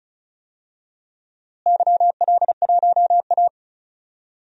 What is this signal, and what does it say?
Morse code, heard as a keyed tone, about 36 wpm, 700 Hz. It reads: YL1A
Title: DXCC-100 Call Signs - 4 Characters - Repeated 2x - 35wpm